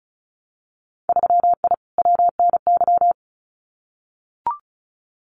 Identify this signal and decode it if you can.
Morse code, heard as a keyed tone, about 36 wpm, 700 Hz. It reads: 3I WNY
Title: Winter Field Day Exchanges Encore 35wpm